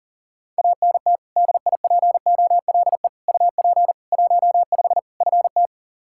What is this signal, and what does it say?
Morse code, heard as a keyed tone, about 40 wpm, 700 Hz. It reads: ANT DIPOLE UP 15 FT
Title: Practice QSO - Exchange 2 - Info - 40wpm